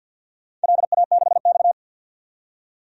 Morse code, heard as a keyed tone, about 51 wpm, 700 Hz. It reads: LA6X
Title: DXCC-100 Call Signs - 4 Characters - 50wpm